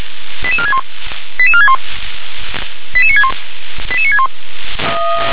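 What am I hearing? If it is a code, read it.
Five-tone selective call (ZVEI), heard as quick runs of five tones, tones of about 70 ms. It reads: 88461 88461 88961 88961
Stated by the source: ZVEI five-tone selective call received by SDR.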